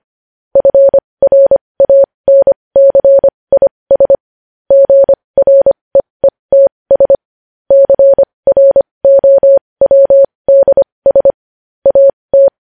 Morse code, heard as a keyed tone, about 25 wpm, spaced as for 20 wpm, 550 Hz. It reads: FRANCIS GREETS CROWDS AT
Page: news headlines in morse code at 20 wpm